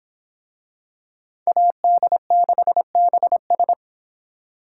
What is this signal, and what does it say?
Morse code, heard as a keyed tone, about 24 wpm, 700 Hz. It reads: AD6BS